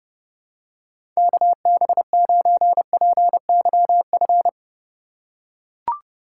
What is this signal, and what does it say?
Morse code, heard as a keyed tone, about 30 wpm, 700 Hz. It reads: KB9PYF
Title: USA Call Signs - 6 Characters - 2x Character Spacing - 30wpm